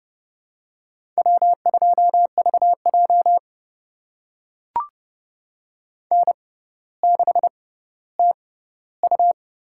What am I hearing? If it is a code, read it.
Morse code, heard as a keyed tone, about 30 wpm, spaced as for 15 wpm, 690 Hz. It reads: W2VJ N6TU
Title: USA Call Signs - 4 Characters - 6x Character Spacing - 30wpm